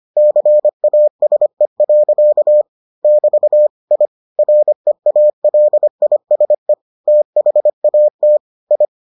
Morse code, heard as a keyed tone, about 25 wpm, 600 Hz. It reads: CASE. <BT> I REALISE THAT I